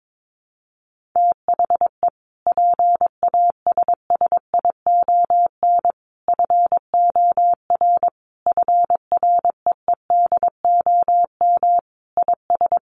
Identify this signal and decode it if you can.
Morse code, heard as a keyed tone, about 22 wpm, 700 Hz. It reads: THE PASSION FOR FREEDOM IS
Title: Sentences from Top 1300 Words 22wpm